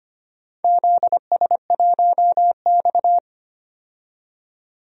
Morse code, heard as a keyed tone, about 25 wpm, 700 Hz. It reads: ZS1X
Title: DXCC-100 Call Signs - 4 Characters - 25wpm